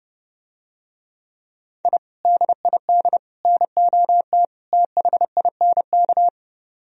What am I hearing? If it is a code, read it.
Morse code, heard as a keyed tone, about 30 wpm, 700 Hz. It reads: I DID NOT THINK